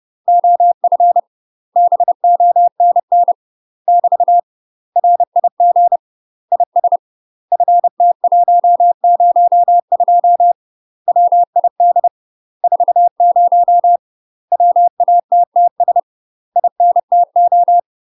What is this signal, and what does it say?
Morse code, heard as a keyed tone, about 30 wpm, 700 Hz. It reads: OF BONN <BT> RIG IS FT102 WID 40 WATTS INTO
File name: UK MASTER QSOs TRAINING / 30WPM / UK MASTER QSOs 0047